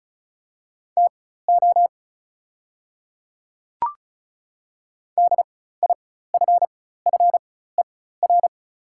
Morse code, heard as a keyed tone, about 34 wpm, spaced as for 16 wpm, 690 Hz. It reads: TO DIFFER
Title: 4x Character Spacing - Top 500 Words - 35wpm